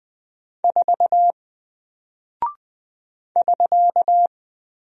Morse code, heard as a keyed tone, about 20 wpm, 700 Hz. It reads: 4 <SK>
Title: Single Character (Letter:Number:Punctuation:Prosign) Alt 20wpm